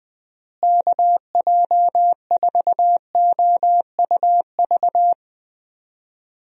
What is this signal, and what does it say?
Morse code, heard as a keyed tone, about 20 wpm, 700 Hz. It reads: KJ4OUV
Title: USA Call Signs - 6 Characters - Repeated 3x - 20wpm